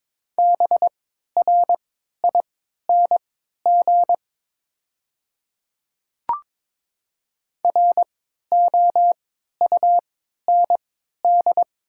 Morse code, heard as a keyed tone, about 22 wpm, spaced as for 13 wpm, 700 Hz. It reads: BRING ROUND
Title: 3x Character Spacing - Top 500 Words - 22wpm